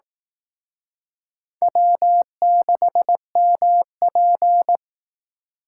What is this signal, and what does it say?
Morse code, heard as a keyed tone, about 18 wpm, 700 Hz. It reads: W6MP
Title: Speed Racing - USA 1x2 Call Signs 15wpm